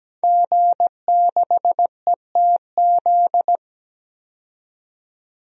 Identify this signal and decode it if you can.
Morse code, heard as a keyed tone, about 17 wpm, 700 Hz. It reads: G6ETZ
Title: DXCC-100 Call Signs - 5 Characters - 17wpm